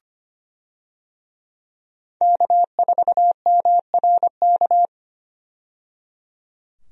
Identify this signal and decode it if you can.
Morse code, heard as a keyed tone, about 25 wpm, 690 Hz. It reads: K4MRK
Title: USA Call Signs - 5 Characters - 25wpm